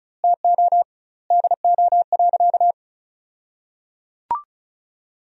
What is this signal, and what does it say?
Morse code, heard as a keyed tone, about 36 wpm, 700 Hz. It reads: TO DO.
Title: 2x Word-Spacing - Sentences from Top 100 Words - 35wpm